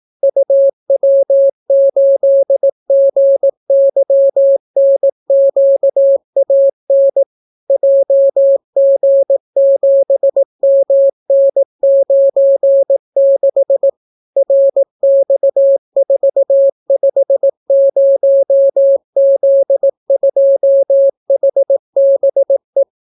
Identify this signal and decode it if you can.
Morse code, heard as a keyed tone, about 18 wpm, 550 Hz. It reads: UW8GYNQAN JG7MN96 RX450Z2HBE